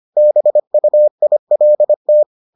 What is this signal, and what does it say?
Morse code, heard as a keyed tone, about 25 wpm, 600 Hz. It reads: BUILT